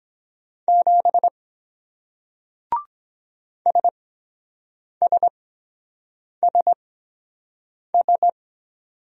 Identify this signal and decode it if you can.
Morse code, heard as a keyed tone, about 23 wpm, spaced as for 7 wpm, 700 Hz. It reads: 7 SSSS